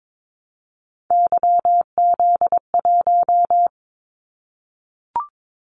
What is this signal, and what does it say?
Morse code, heard as a keyed tone, about 22 wpm, 700 Hz. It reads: YZ1
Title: Callsign Prefixes for Top 25 DXCC - 22wpm